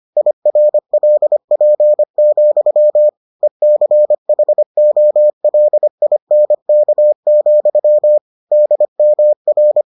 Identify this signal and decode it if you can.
Morse code, heard as a keyed tone, about 25 wpm, 600 Hz. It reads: IRLP, ECHOLINK, DMR